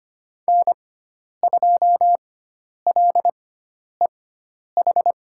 Morse code, heard as a keyed tone, about 25 wpm, spaced as for 10 wpm, 700 Hz. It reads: N2LEH
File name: Lesson 16g - Letter c - Call Signs 25wpm